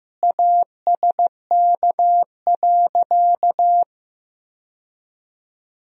Morse code, heard as a keyed tone, about 15 wpm, 700 Hz. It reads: ASK.
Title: Sentences from Top 1000 Words - Review Part 1 - 15wpm